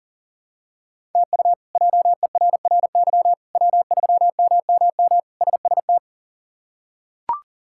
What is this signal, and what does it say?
Morse code, heard as a keyed tone, about 37 wpm, 700 Hz. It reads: TU JERRY W3MMM SST
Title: K1USN SST QSO Practice - 40wpm